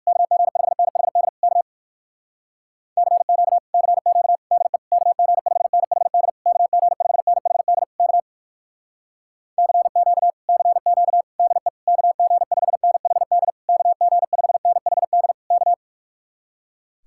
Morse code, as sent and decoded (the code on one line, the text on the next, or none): -.- --. ..... -. .... -.. / -.- / -.-. --.- / -.-. --.- / -.. / -.- --. ..... -. .... -.. / -.- --. ..... -. .... -.. / -.- / -.-. --.- / -.-. --.- / -.. / -.- --. ..... -. .... -.. / -.- --. ..... -. .... -.. / -.-
KG5NHD K CQ CQ DE KG5NHD KG5NHD K CQ CQ DE KG5NHD KG5NHD K